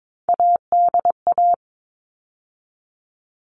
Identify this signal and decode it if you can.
Morse code, heard as a keyed tone, about 22 wpm, 700 Hz. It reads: ADA
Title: All Cut Numbers - 3-Digit 22wpm